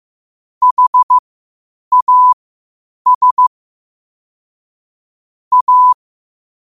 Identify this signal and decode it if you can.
Morse code, heard as a keyed tone, about 15 wpm, spaced as for 8 wpm, 1000 Hz. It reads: HAS A
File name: qotdfiles / 8wpm